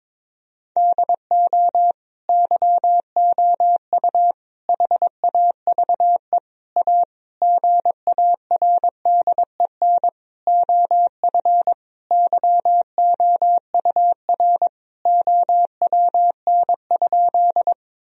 Morse code, heard as a keyed tone, about 22 wpm, 700 Hz. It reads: DO YOU HAVE A GARDEN OF YOUR OWN?
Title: Sentences from Top 1000 Words - Review Part 4 - 22wpm